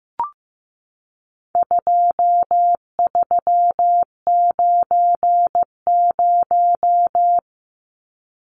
Morse code, heard as a keyed tone, about 15 wpm, 700 Hz. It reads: 2390
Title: Numbers - 4-Digit 15wpm